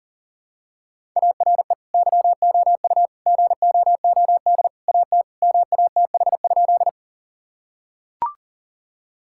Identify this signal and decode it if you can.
Morse code, heard as a keyed tone, about 40 wpm, 700 Hz. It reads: ARE YOU GOOD AT MATH?
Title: Sentences from Top 1000 Words - Review Part 6 - 40wpm